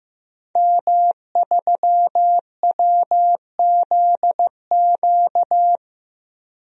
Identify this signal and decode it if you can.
Morse code, heard as a keyed tone, about 15 wpm, 700 Hz. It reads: M3WZQ
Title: DXCC-100 Call Signs - 5 Characters - Repeated 2x - 15wpm